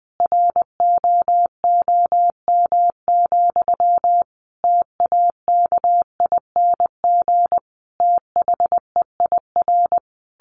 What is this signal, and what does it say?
Morse code, heard as a keyed tone, about 20 wpm, 700 Hz. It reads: ROOM, TAKING THEIR
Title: A Sound Of Thunder 20wpm